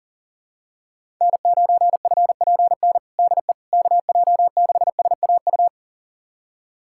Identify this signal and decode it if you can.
Morse code, heard as a keyed tone, about 40 wpm, 700 Hz. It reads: N9FPN DE KJ6SAU
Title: USA Callsign DE Callsign 40wpm